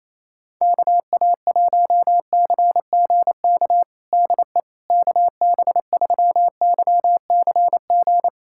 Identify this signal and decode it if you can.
Morse code, heard as a keyed tone, about 28 wpm, 700 Hz. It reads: KA1CGK DE KB3YCG
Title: USA Callsign DE Callsign - 6 Characters - 28wpm